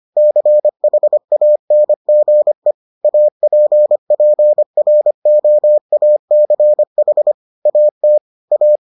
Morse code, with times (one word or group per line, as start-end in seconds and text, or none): 0.2-2.7 CHANGE
3.0-7.3 APPROACH
7.7-8.2 AT
8.5-8.8 A